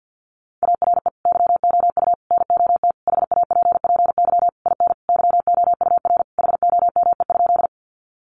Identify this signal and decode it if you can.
Morse code, heard as a keyed tone, about 50 wpm, 700 Hz. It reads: ARE YOU NOT HAPPY IN YOUR HOME?
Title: Sentences from Top 1000 Words 50wpm